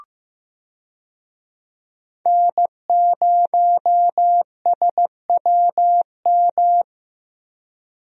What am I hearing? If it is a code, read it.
Morse code, heard as a keyed tone, about 15 wpm, 700 Hz. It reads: N0SWM
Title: USA Call Signs - 5 Characters - Repeated 2x - 15wpm